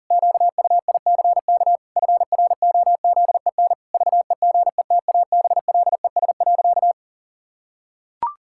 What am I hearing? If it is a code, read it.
Morse code, heard as a keyed tone, about 37 wpm, 700 Hz. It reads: QUICK FROZEN VEGETABLES.
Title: Pangram Challenge v2 - 40wpm